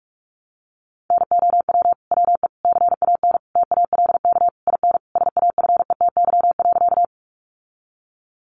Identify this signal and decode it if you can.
Morse code, heard as a keyed tone, about 44 wpm, 700 Hz. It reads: NOW WE CAN TALK IN SAFETY.
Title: Sentences from Top 700 Words 45wpm